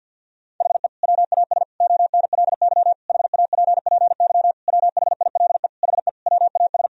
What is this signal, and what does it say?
Morse code, heard as a keyed tone, about 50 wpm, 700 Hz. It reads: HE WAS ONLY HAPPY WHILE HE WAS
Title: Sentences from Top 1000 Words - Review Part 4 - 50wpm